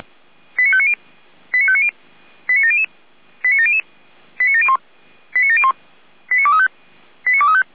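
Five-tone selective call (ZVEI), heard as quick runs of five tones, tones of about 70 ms. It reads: 89590 89590 89700 89700 89701 89701 89235 89235
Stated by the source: ZVEI five-tone selective call received by SDR.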